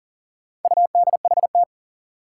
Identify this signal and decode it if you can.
Morse code, heard as a keyed tone, about 40 wpm, 700 Hz. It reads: UDHT
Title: Four Challenging Letter-Number Combinations 40wpm